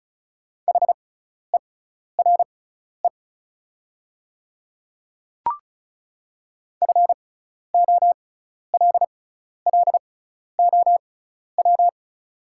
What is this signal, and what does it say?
Morse code, heard as a keyed tone, about 36 wpm, spaced as for 12 wpm, 700 Hz. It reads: HERE FOLLOW
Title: 6x Character Spacing - Top 200 Words - 35wpm